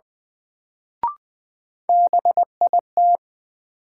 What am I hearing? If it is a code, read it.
Morse code, heard as a keyed tone, about 21 wpm, 700 Hz. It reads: BIT
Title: Sets of 3 Words - New Word Drill - 20wpm